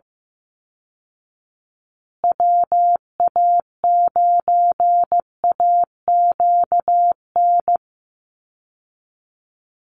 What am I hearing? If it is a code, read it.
Morse code, heard as a keyed tone, about 15 wpm, 700 Hz. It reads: WA9AQN
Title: Call Signs Repeated 2x - 15wpm